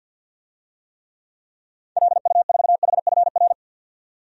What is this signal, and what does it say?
Morse code, heard as a keyed tone, about 47 wpm, 700 Hz. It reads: LU4HFR
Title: DXCC-100 Call Signs - 6 Characters - 50wpm